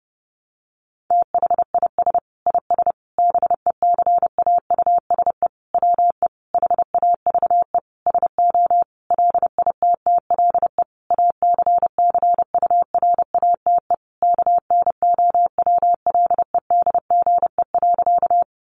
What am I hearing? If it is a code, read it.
Morse code, heard as a keyed tone, about 28 wpm, 700 Hz. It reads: THIS IS BECAUSE WE HAVE SO LITTLE ACCURATE KNOWLEDGE.
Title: Sentences from Top 1200 Words 30wpm